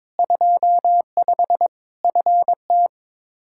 Morse code, heard as a keyed tone, about 22 wpm, 700 Hz. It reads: 25 FT
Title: K1USN SST QSO Practice 22wpm